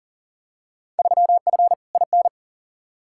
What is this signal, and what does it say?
Morse code, heard as a keyed tone, about 40 wpm, 700 Hz. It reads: 3F IN
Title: Field Day Exchanges - 40wpm